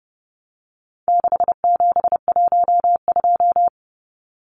Morse code, heard as a keyed tone, about 28 wpm, 700 Hz. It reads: 6712
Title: Numbers - 4-Digit 30wpm